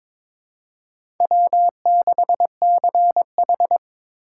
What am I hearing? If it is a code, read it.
Morse code, heard as a keyed tone, about 22 wpm, 700 Hz. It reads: W6CH
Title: USA Call Signs - 4 Characters - 4x Character Spacing - 22wpm